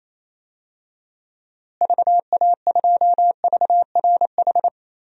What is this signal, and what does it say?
Morse code, heard as a keyed tone, about 28 wpm, 700 Hz. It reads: VA2VRH
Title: DXCC-100 Call Signs - 6 Characters - Repeated 3x - 28wpm